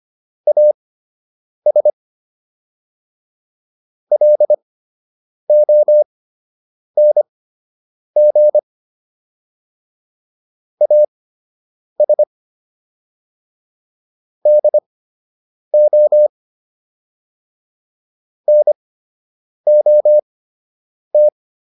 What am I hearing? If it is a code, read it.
Morse code, heard as a keyed tone, about 25 wpm, spaced as for 8 wpm, 600 Hz. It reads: AS LONG AS DO NOT